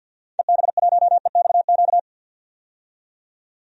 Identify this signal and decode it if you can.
Morse code, heard as a keyed tone, about 52 wpm, 700 Hz. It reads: EB1EXQ
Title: DXCC-100 Call Signs - 6 Characters - 50wpm